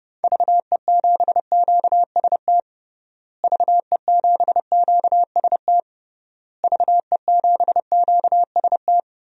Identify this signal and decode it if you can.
Morse code, heard as a keyed tone, about 30 wpm, 700 Hz. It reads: VE7QST VE7QST VE7QST
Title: DXCC-100 Call Signs - 6 Characters - Repeated 3x - 30wpm